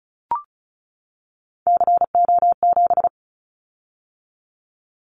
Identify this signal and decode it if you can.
Morse code, heard as a keyed tone, about 36 wpm, 700 Hz. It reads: CO7
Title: Callsign Prefixes for Top 100 DXCC - 35wpm